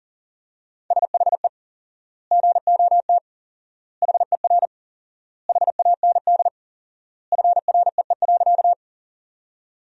Morse code, heard as a keyed tone, about 40 wpm, 700 Hz. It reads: SHE GOT HER HAND FREE.
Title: 4x Word-Spacing - Sentences from Top 500 Words - 40wpm